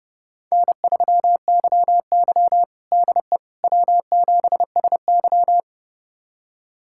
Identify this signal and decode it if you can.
Morse code, heard as a keyed tone, about 29 wpm, 700 Hz. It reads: N3YY DE W7SY
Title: USA Callsign DE Callsign - 4 Characters - 30wpm